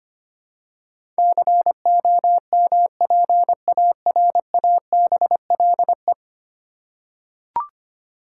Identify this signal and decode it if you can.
Morse code, heard as a keyed tone, about 25 wpm, 700 Hz. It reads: COMPARABLE
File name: 10 Letter Words - First 3 Letters Given - 25wpm